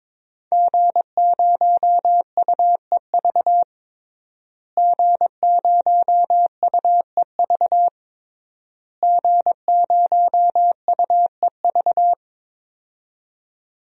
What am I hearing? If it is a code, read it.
Morse code, heard as a keyed tone, about 22 wpm, 700 Hz. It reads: G0UEV G0UEV G0UEV
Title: DXCC-100 Call Signs - 5 Characters - Repeated 3x - 22wpm